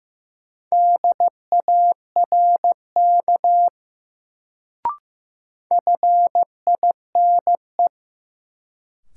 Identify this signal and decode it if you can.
Morse code, heard as a keyed tone, about 15 wpm, 700 Hz. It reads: DARK FINE
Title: Sentences from Top 500 Words - New Word Drill - 15wpm